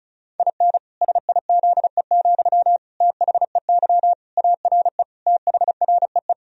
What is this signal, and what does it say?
Morse code, heard as a keyed tone, about 36 wpm, 700 Hz. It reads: IN SIZE, THEY ARE THREE
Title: Sentences from Top 1000 Words - Review Part 2 - 35wpm